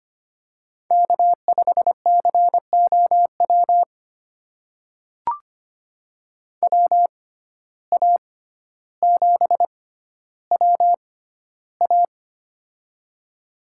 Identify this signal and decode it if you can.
Morse code, heard as a keyed tone, about 25 wpm, spaced as for 13 wpm, 700 Hz. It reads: K5COW WA7WA
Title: USA Call Signs - 5 Characters - 6x Character Spacing - 25wpm